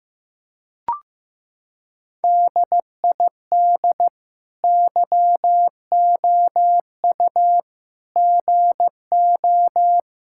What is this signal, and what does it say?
Morse code, heard as a keyed tone, about 15 wpm, 700 Hz. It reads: DID YOU GO
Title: Sentences from Top 1000 Words - Review Part 5 - 15wpm